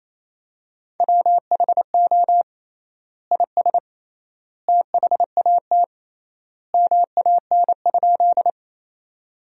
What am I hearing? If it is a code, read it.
Morse code, heard as a keyed tone, about 28 wpm, 700 Hz. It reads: WHO IS THAT MAN?
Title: 3x Word-Spacing - Sentences from Top 200 Words - 28wpm